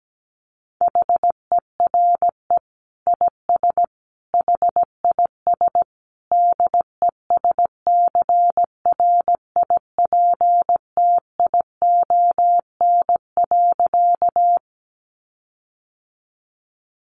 Morse code, heard as a keyed tone, about 17 wpm, 700 Hz. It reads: HERE IS HIS DESCRIPTION.
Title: Sentences from Top 1300 Words 17wpm